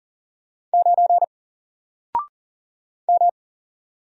Morse code, heard as a keyed tone, about 40 wpm, 700 Hz. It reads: 9 M
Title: Single Character (Letter:Number:Punctuation:Prosign) 40wpm